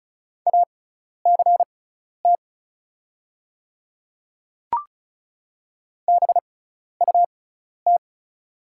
Morse code, heard as a keyed tone, about 36 wpm, spaced as for 12 wpm, 690 Hz. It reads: ACT BUT
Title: 6x Character Spacing - Top 500 Words - 35wpm